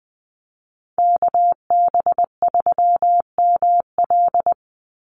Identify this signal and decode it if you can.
Morse code, heard as a keyed tone, about 20 wpm, 700 Hz. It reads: KB3ML
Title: Call Signs Repeated 3x - 20wpm